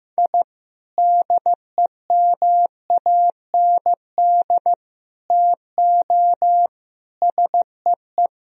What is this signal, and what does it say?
Morse code, heard as a keyed tone, about 15 wpm, 700 Hz. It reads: I DEMAND TO SEE
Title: Sentences from Top 1000 Words - Review Part 4 - 15wpm